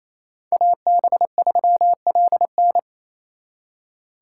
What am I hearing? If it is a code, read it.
Morse code, heard as a keyed tone, about 33 wpm, 700 Hz. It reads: AB3LN
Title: Speed Racing - USA 2x2 Call Signs 28wpm